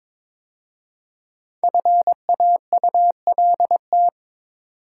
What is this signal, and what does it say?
Morse code, heard as a keyed tone, about 22 wpm, 700 Hz. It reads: FAULT
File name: 5 Letter Words - First 3 Letters Given - 22wpm